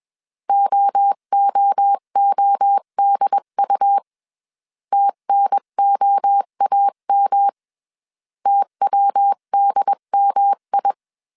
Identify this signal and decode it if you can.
Morse code, heard as a keyed tone, about 21 wpm, 800 Hz. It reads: OOODU TNOAM TWDMI
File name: CW / DET / 1 3 1500C S B 20wpm 1754